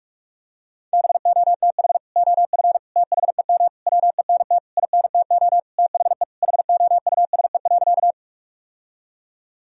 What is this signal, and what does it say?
Morse code, heard as a keyed tone, about 46 wpm, 700 Hz. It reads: BOTH OF THEM WENT INTO THE HOUSE.
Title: Sentences from Top 1000 Words - Review Part 1 - 45wpm